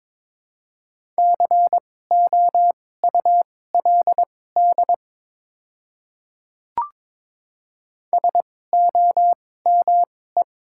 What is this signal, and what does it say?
Morse code, heard as a keyed tone, about 22 wpm, spaced as for 16 wpm, 700 Hz. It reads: COULD SOME
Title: 2x Character Spacing - Top 100 Words - 22wpm